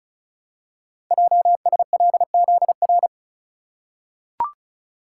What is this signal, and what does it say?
Morse code, heard as a keyed tone, about 36 wpm, 700 Hz. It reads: JSLZR
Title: Code Groups - 5 Letters - 35wpm